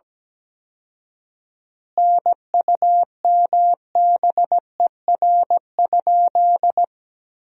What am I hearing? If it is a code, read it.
Morse code, heard as a keyed tone, about 17 wpm, 700 Hz. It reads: NUMBER?
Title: 4x Word-Spacing - Sentences from Top 100 Words - 17wpm